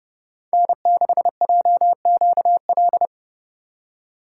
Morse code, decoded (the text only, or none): N6JQL